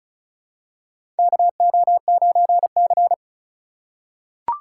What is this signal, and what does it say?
Morse code, heard as a keyed tone, about 36 wpm, 700 Hz. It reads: KO9C
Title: USA Call Signs - 8x Character Spacing - 35wpm